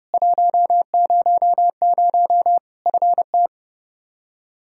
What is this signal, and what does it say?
Morse code, heard as a keyed tone, about 30 wpm, 700 Hz. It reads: 100 FT
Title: K1USN SST QSO Practice 30wpm